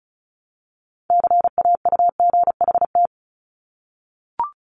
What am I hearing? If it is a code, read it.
Morse code, heard as a keyed tone, about 36 wpm, 700 Hz. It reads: CAUGHT
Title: Multi-letter Phonemes Repeated 3x 35wpm